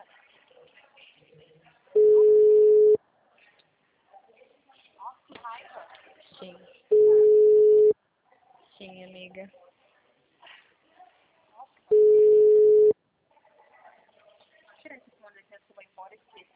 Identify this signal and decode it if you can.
Ringback tone (Europe, 425 Hz), with speech.
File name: out- -2009- - - .243